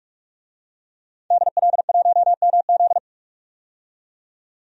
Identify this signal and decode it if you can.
Morse code, heard as a keyed tone, about 46 wpm, 700 Hz. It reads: DL1MZ
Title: DXCC-100 Call Signs - 5 Characters - 45wpm